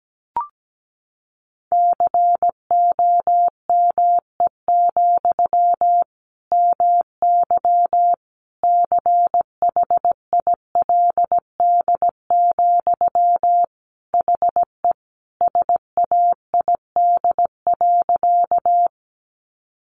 Morse code, heard as a keyed tone, about 17 wpm, 700 Hz. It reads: COME, MY CHILD, HE SAID.
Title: Sentences from Top 800 Words 17wpm